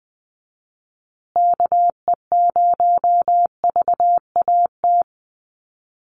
Morse code, heard as a keyed tone, about 20 wpm, 700 Hz. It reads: KE0VAT
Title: USA Call Signs Repeated 2x 20wpm